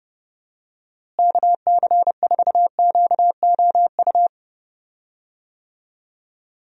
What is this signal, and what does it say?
Morse code, heard as a keyed tone, about 31 wpm, 700 Hz. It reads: KC4QOU
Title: USA Call Signs - 6 Characters - Repeated 2x - 30wpm